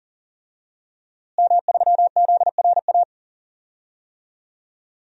Morse code, heard as a keyed tone, about 40 wpm, 700 Hz. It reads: M3ZRA
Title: DXCC-100 Call Signs - 5 Characters - Repeated 2x - 40wpm